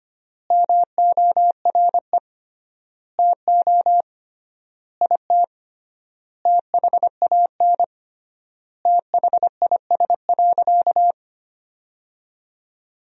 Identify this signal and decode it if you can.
Morse code, heard as a keyed tone, about 25 wpm, 700 Hz. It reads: MORE TO IT THAN THIS.
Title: 3x Word-Spacing - Sentences from Top 100 Words - 25wpm